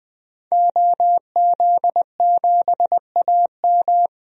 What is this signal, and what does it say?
Morse code, heard as a keyed tone, about 20 wpm, 700 Hz. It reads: OZ7AM
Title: DXCC-100 Call Signs - 5 Characters - 20wpm